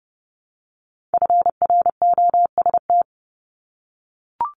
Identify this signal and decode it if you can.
Morse code, heard as a keyed tone, about 30 wpm, 700 Hz. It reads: FROST
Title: Multi-letter Phonemes Repeated 3x 30wpm